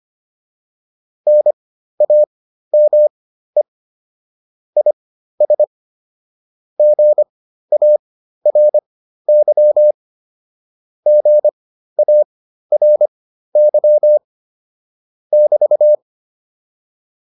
Morse code, heard as a keyed tone, about 25 wpm, spaced as for 13 wpm, 600 Hz. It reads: NAME IS GARY GARY <BT>